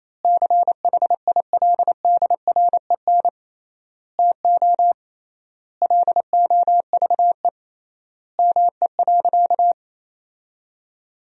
Morse code, heard as a keyed tone, about 28 wpm, 700 Hz. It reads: CHILDREN TO LOVE ME.
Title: 3x Word-Spacing - Sentences from Top 400 Words - 28wpm